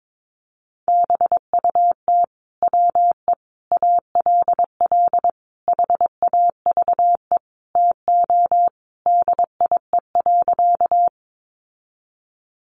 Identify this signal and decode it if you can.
Morse code, heard as a keyed tone, about 22 wpm, 700 Hz. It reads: BUT WE ALL HAVE TO DIE.
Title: Sentences from Top 1300 Words 22wpm